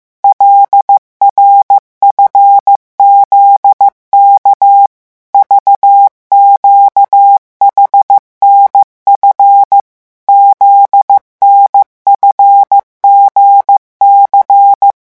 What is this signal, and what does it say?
Morse code, heard as a keyed tone, about 15 wpm, 800 Hz. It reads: LRFZK VQHNF ZNFGC
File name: morse mp3 / revision8-vitesse 14 mots